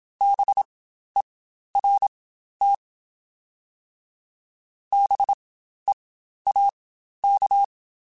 Morse code, heard as a keyed tone, about 27 wpm, spaced as for 12 wpm, 800 Hz. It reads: BERT BEAK
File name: Spaced Alphas / Spaced Alphas mp3s / Bs sp @25